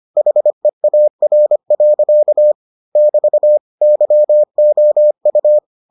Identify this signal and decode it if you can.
Morse code, heard as a keyed tone, about 25 wpm, 600 Hz. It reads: HEAR. <BT> YOU